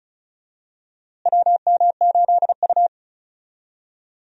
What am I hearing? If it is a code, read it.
Morse code, heard as a keyed tone, about 34 wpm, 700 Hz. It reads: WM8U
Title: USA Call Signs - 4 Characters - Repeated 3x - 35wpm